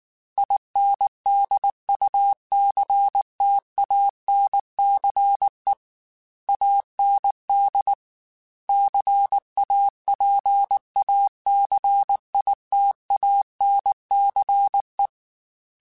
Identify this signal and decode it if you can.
Morse code, heard as a keyed tone, about 19 wpm, 800 Hz. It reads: INDUCTANCE AND CAPACITANCE
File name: Technical Subjects / Technical Subjects-mp3s / Reactance @18